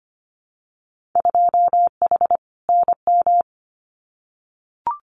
Morse code, heard as a keyed tone, about 25 wpm, 700 Hz. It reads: 2H NM
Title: Winter Field Day Exchanges Encore 25wpm